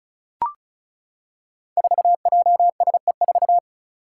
Morse code, heard as a keyed tone, about 36 wpm, 700 Hz. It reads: 4JSE4
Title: Five Challenging Letter-Number Combinations 35wpm